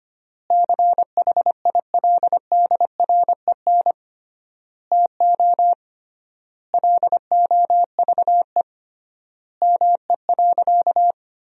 Morse code, heard as a keyed tone, about 25 wpm, 700 Hz. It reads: CHILDREN TO LOVE ME.
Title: 3x Word-Spacing - Sentences from Top 400 Words - 25wpm